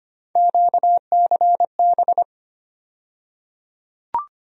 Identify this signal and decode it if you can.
Morse code, heard as a keyed tone, about 25 wpm, 700 Hz. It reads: QCB
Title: USA Call Sign Postfixes - 3 Letters 25wpm